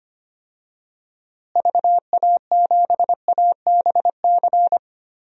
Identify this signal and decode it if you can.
Morse code, heard as a keyed tone, about 25 wpm, 700 Hz. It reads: VA7ABC
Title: DXCC-100 Call Signs - 6 Characters - Repeated 2x - 25wpm